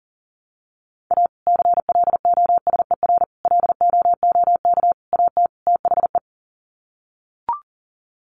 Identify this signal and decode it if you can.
Morse code, heard as a keyed tone, about 40 wpm, 700 Hz. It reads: A CLOSER LOOK AT THE
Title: Sets of 5 Words Encore 40wpm